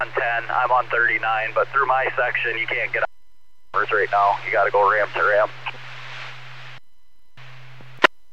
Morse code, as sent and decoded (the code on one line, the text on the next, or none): none